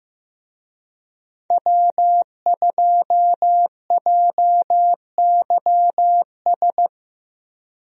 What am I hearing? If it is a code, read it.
Morse code, heard as a keyed tone, about 15 wpm, 700 Hz. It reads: W2JYS
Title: USA Call Signs - 5 Characters - Repeated 2x - 15wpm